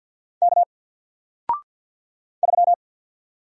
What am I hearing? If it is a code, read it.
Morse code, heard as a keyed tone, about 50 wpm, 700 Hz. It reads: K 3
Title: Single Character (Letter:Number:Punctuation:Prosign) Alt 50wpm